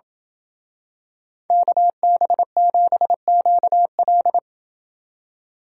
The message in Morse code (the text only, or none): KB7QL